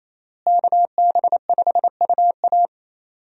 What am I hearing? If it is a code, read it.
Morse code, heard as a keyed tone, about 28 wpm, 700 Hz. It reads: KB5UA
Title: USA Call Signs - 5 Characters - Repeated 3x - 28wpm